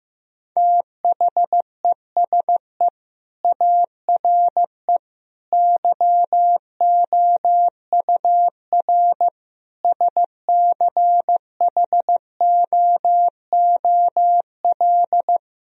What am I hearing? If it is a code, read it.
Morse code, heard as a keyed tone, about 15 wpm, 700 Hz. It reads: THESE ARE YOUR SCHOOL
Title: Sentences from Top 1000 Words - Review Part 1 - 15wpm